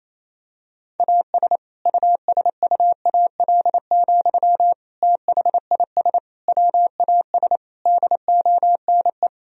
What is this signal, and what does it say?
Morse code, heard as a keyed tone, about 28 wpm, 700 Hz. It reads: AS USUAL, THIS WAS DONE
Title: Sentences from Top 1000 Words - Review Part 2 - 28wpm